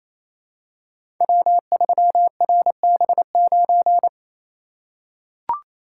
Morse code, heard as a keyed tone, about 28 wpm, 690 Hz. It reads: W3RB9
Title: Code Groups - 5 Letter:Numbers - 28wpm